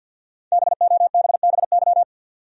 Morse code, heard as a keyed tone, about 52 wpm, 700 Hz. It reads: BOBBY
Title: USA Common Names - 50wpm